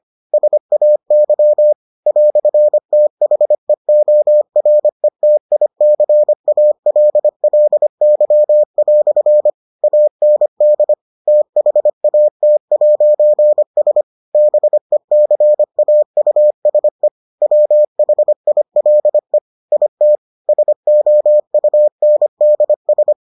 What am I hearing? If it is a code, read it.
Morse code, heard as a keyed tone, about 25 wpm, 600 Hz. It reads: SAY "THEORETICALLY" AND THAT'S BECAUSE WHILE IT SOUNDS